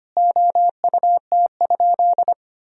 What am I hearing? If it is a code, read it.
Morse code, heard as a keyed tone, about 25 wpm, 700 Hz. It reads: OUT?
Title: Sentences from Top 1000 Words - Review Part 3 - 25wpm